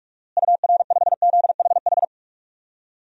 Morse code, heard as a keyed tone, about 46 wpm, 700 Hz. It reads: UR5ZHH
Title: DXCC-100 Call Signs - 6 Characters - Repeated 3x - 45wpm